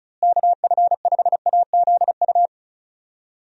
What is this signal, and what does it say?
Morse code, heard as a keyed tone, about 36 wpm, 700 Hz. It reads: KF5AZU
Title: USA Call Signs - 6 Characters - Repeated 2x - 35wpm